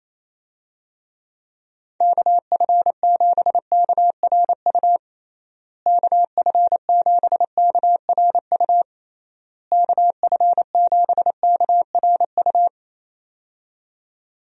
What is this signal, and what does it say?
Morse code, heard as a keyed tone, about 28 wpm, 700 Hz. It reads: KF7KRU KF7KRU KF7KRU
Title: USA Call Signs - 6 Characters - Repeated 3x - 28wpm